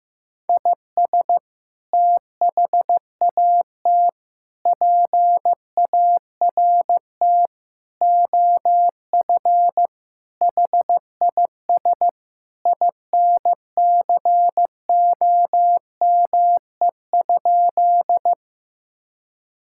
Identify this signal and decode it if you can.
Morse code, heard as a keyed tone, about 15 wpm, 700 Hz. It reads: IS THAT PART OF HIS INCOME?
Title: Sentences from Top 1000 Words - Review Part 5 - 15wpm